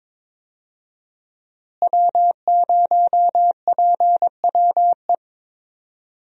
Morse code, heard as a keyed tone, about 22 wpm, 700 Hz. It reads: W0PWE
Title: USA Call Signs - 5 Characters - Repeated 3x - 22wpm